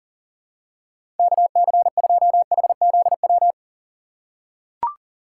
Morse code, heard as a keyed tone, about 40 wpm, 700 Hz. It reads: KC2HZW
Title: USA Call Signs - 6 Characters - 6x Character Spacing - 40wpm